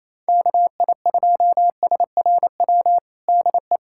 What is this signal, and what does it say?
Morse code, heard as a keyed tone, about 28 wpm, 700 Hz. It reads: KI2SRW DE
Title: USA Callsign DE Callsign 28wpm